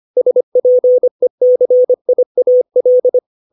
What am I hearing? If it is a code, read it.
Morse code, heard as a keyed tone, about 25 wpm, 500 Hz. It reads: SPECIAL
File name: 632 words / 632 Words 25WPM